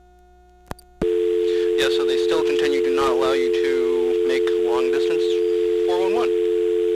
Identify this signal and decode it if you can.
Dial tone (North America), 350 + 440 Hz, with speech.
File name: blackratchet / recordings / Verizon411